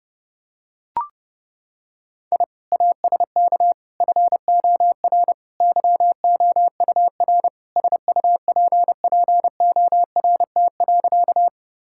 Morse code, heard as a keyed tone, about 32 wpm, 700 Hz. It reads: I ASK FOR YOUR SUPPORT.
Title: Sentences from Top 1000 Words - Review Part 5 - 30wpm